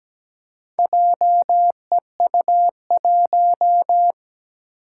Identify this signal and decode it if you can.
Morse code, heard as a keyed tone, about 17 wpm, 700 Hz. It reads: JEU1
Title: Four Challenging Letter-Number Combinations 17wpm